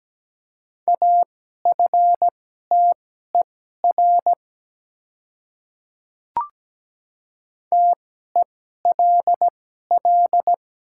Morse code, heard as a keyed tone, about 17 wpm, spaced as for 12 wpm, 700 Hz. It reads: AFTER TELL
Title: 2x Character Spacing - Top 200 Words - 17wpm